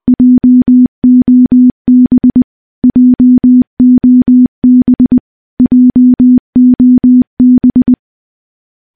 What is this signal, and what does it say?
Morse code, heard as a keyed tone, about 19 wpm, 250 Hz. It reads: JOB JOB JOB
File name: morse / morse-training-20WPM